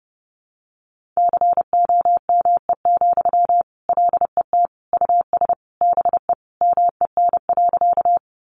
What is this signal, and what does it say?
Morse code, heard as a keyed tone, about 30 wpm, 700 Hz. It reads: COME, LET US BE MEN.
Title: Sentences from Top 300 Words 30wpm